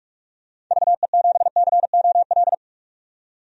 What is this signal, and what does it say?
Morse code, heard as a keyed tone, about 44 wpm, 700 Hz. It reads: VE7COL
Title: DXCC-100 Call Signs - 6 Characters - 45wpm